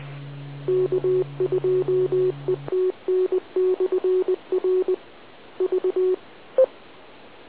Morse code, heard as a keyed tone, about 20 wpm, 370 Hz. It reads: K2ETN/R V